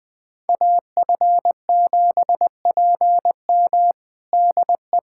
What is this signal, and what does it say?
Morse code, heard as a keyed tone, about 20 wpm, 700 Hz. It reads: AF7PM DE
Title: USA Callsign DE Callsign - 5 Characters - 20wpm